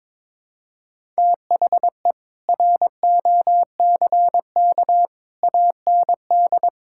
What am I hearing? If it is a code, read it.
Morse code, heard as a keyed tone, about 22 wpm, 700 Hz. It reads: THE ROCK AND
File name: Sentences with 7-Word Phrases - 22wpm